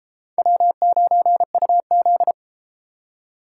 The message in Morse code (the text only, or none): W9UZ